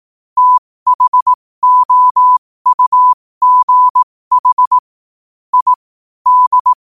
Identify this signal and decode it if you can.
Morse code, heard as a keyed tone, about 18 wpm, spaced as for 15 wpm, 1000 Hz. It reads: THOUGH I D
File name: qotdfiles / 15wpm